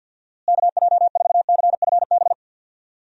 Morse code, heard as a keyed tone, about 51 wpm, 700 Hz. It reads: KJ4CFB
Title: USA Call Signs - 6 Characters - Repeated 2x - 50wpm